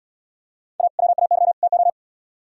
Morse code, heard as a keyed tone, about 72 wpm, 700 Hz. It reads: A WAY TO